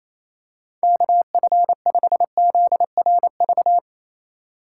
Morse code, heard as a keyed tone, about 28 wpm, 700 Hz. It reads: KF5ZRV
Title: USA Call Signs - 6 Characters - 3x Character Spacing - 28wpm